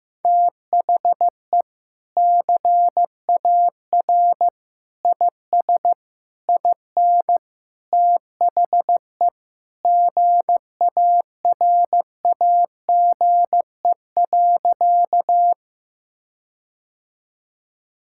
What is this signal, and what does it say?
Morse code, heard as a keyed tone, about 16 wpm, 700 Hz. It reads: THE CAR IS IN THE GARAGE.
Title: Sentences from Top 1600 Words 15wpm